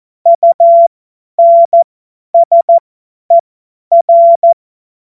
Morse code, heard as a keyed tone, about 14 wpm, spaced as for 10 wpm, 680 Hz. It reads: UNSER
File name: Morsekurs / Witze / Witz 22. Februar 2022 50 F70